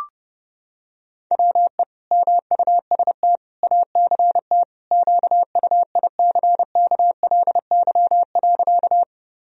Morse code, heard as a keyed tone, about 30 wpm, 700 Hz. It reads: WE MUST ACT QUICKLY.
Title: Sentences from Top 1000 Words - Review Part 1 - 30wpm